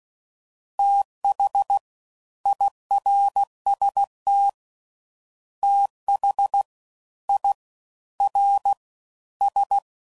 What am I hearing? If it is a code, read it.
Morse code, heard as a keyed tone, about 16 wpm, 800 Hz. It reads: TH IRST TH I R S
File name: Prefixes / FLs & THs / Fast TH